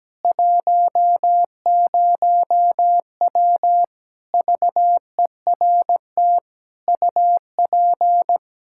Morse code, heard as a keyed tone, about 17 wpm, 700 Hz. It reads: 10W VERT UP
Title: Practice QSO - Exchange 2 - Info - 17wpm